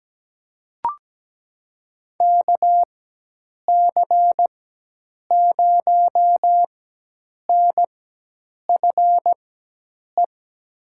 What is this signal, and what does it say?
Morse code, heard as a keyed tone, about 17 wpm, spaced as for 8 wpm, 700 Hz. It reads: KC0NFE
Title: USA Call Signs - 6 Characters - 4x Character Spacing - 17wpm